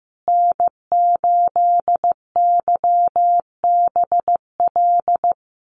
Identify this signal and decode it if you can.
Morse code, heard as a keyed tone, about 15 wpm, 700 Hz. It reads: N8YBL
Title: USA Call Signs Repeated 3x 15wpm